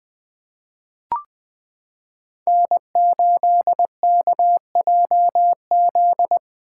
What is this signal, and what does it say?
Morse code, heard as a keyed tone, about 20 wpm, 700 Hz. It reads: N8KJZ
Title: Code Groups - 5 Letter:Numbers - 20wpm